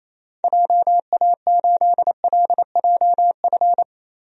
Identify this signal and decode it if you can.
Morse code, heard as a keyed tone, about 28 wpm, 700 Hz. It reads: JA8LJF
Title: DXCC-100 Call Signs - 6 Characters - Repeated 3x - 28wpm